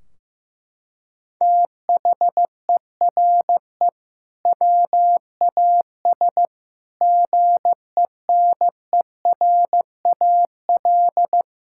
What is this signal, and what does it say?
Morse code, heard as a keyed tone, about 15 wpm, 700 Hz. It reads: THERE WAS GENERAL
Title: Sentences from Top 1600 Words 15wpm